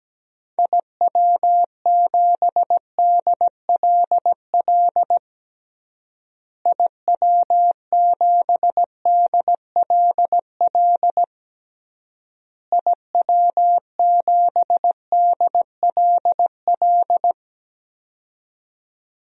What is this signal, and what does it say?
Morse code, heard as a keyed tone, about 17 wpm, 700 Hz. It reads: IW7DLL IW7DLL IW7DLL
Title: DXCC-100 Call Signs - 6 Characters - Repeated 3x - 17wpm